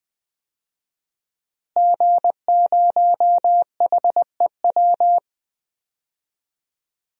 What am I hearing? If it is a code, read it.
Morse code, heard as a keyed tone, about 20 wpm, 700 Hz. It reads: G0HEW
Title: DXCC-100 Call Signs - 5 Characters - Repeated 3x - 20wpm